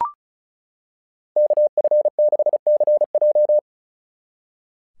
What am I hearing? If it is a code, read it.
Morse code, heard as a keyed tone, about 34 wpm, 600 Hz. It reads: KF6CJ
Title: SOTA QSO Practice 35wpm